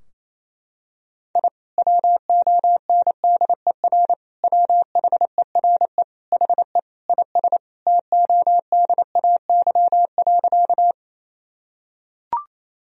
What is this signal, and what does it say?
Morse code, heard as a keyed tone, about 28 wpm, 700 Hz. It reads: I WONDER WHERE HE IS TODAY.
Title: Sentences from Top 1000 Words - Review Part 3 - 28wpm